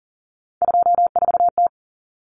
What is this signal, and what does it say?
Morse code, heard as a keyed tone, about 40 wpm, 700 Hz. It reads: 24T
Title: Common Cut Numbers - 3-Digit 40wpm